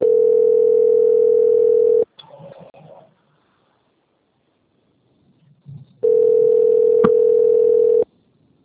Ringback tone (North America), 440 + 480 Hz.